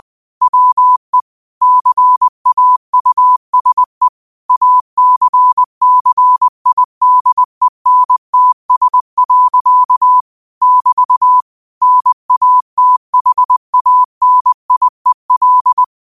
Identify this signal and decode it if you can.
Morse code, heard as a keyed tone, about 20 wpm, 1000 Hz. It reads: WE CAUSE ACCIDENTS. <BT> NATHANIEL
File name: qotdfiles / 20wpm